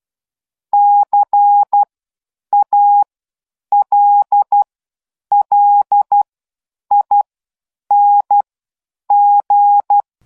Morse code, heard as a keyed tone, about 12 wpm, spaced as for 8 wpm, 820 Hz. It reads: CALLING